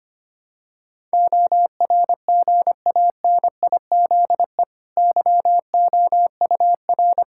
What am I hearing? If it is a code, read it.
Morse code, heard as a keyed tone, about 25 wpm, 700 Hz. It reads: ORGANIZE YOUR
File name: Easy Sentences with Hints - 25wpm